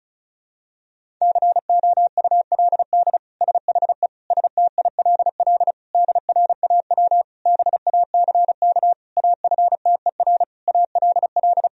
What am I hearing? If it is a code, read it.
Morse code, heard as a keyed tone, about 34 wpm, 700 Hz. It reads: COULD SHE STILL DRAW BACK AFTER ALL
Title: Sentences from Top 1000 Words - Review Part 1 - 35wpm